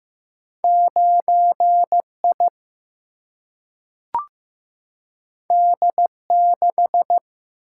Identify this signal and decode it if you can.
Morse code, heard as a keyed tone, about 15 wpm, 700 Hz. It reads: 9I D6
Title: Single-Letter Single-Number 15wpm